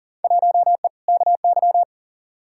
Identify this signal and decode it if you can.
Morse code, heard as a keyed tone, about 39 wpm, 700 Hz. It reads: 1E KY
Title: Field Day Exchanges - 40wpm